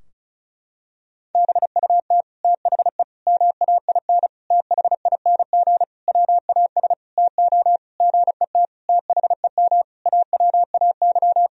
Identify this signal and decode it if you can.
Morse code, heard as a keyed tone, about 35 wpm, 700 Hz. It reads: BUT THE MAIN THING WAS TO GET THEM AWAY
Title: Sentences from Top 1000 Words - Review Part 2 - 35wpm